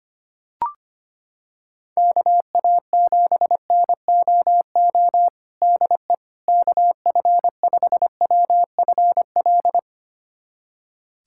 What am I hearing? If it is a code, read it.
Morse code, heard as a keyed tone, about 25 wpm, 700 Hz. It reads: KA7NOO DE KF5WFL
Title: USA Callsign DE Callsign 25wpm